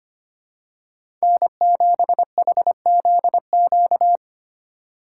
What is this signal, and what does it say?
Morse code, heard as a keyed tone, about 25 wpm, 700 Hz. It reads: N7HZQ
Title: USA Call Signs - 5 Characters - 3x Character Spacing - 25wpm